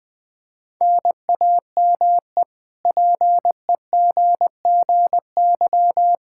Morse code, heard as a keyed tone, about 21 wpm, 700 Hz. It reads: NAME PEGGY
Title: K1USN SST QSO Practice 20wpm